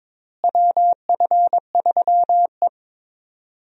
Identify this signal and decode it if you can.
Morse code, heard as a keyed tone, about 21 wpm, 700 Hz. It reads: WF3E
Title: USA Call Signs - 4 Characters - Repeated 2x - 22wpm